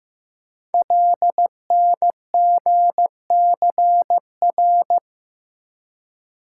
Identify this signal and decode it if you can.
Morse code, heard as a keyed tone, about 15 wpm, 700 Hz. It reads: LNGCR
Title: Code Groups - 5 Letters - 15wpm